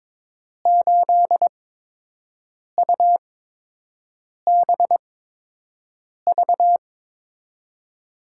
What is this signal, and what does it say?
Morse code, heard as a keyed tone, about 22 wpm, spaced as for 6 wpm, 700 Hz. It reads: 8UBV
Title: USA Call Signs - 6 Characters - 8x Character Spacing - 22wpm